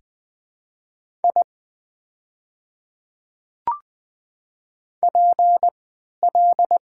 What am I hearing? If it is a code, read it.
Morse code, heard as a keyed tone, about 20 wpm, 700 Hz. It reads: I P L
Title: 3x Character Spacing - Top 200 Words - 20wpm